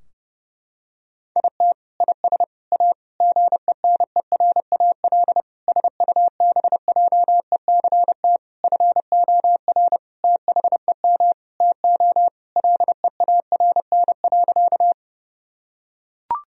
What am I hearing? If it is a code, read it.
Morse code, heard as a keyed tone, about 28 wpm, 700 Hz. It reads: IT IS A GENERAL SUBJECT FOR THEM TO LEARN.
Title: Sentences from Top 1000 Words - Review Part 3 - 30wpm